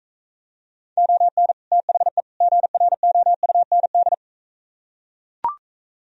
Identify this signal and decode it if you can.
Morse code, heard as a keyed tone, about 44 wpm, 700 Hz. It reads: ON THE GROUND